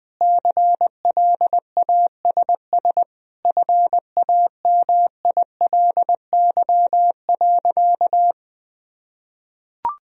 Morse code, heard as a keyed tone, about 21 wpm, 700 Hz. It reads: CLASS FAMILY.
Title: Sentences from Top 1000 Words - Review Part 2 - 20wpm